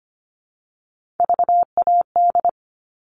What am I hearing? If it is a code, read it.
Morse code, heard as a keyed tone, about 25 wpm, 700 Hz. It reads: VAD
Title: All Cut Numbers - 3-Digit 25wpm